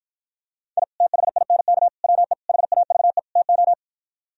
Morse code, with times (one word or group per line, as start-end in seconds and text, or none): 0.8-0.8 I
1.0-1.9 THINK
2.0-2.3 WE
2.5-3.2 HAVE
3.4-3.7 TO